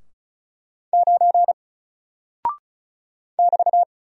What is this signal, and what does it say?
Morse code, heard as a keyed tone, about 36 wpm, 700 Hz. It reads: 9 <BT>
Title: Single Character (Letter:Number:Punctuation:Prosign) 35wpm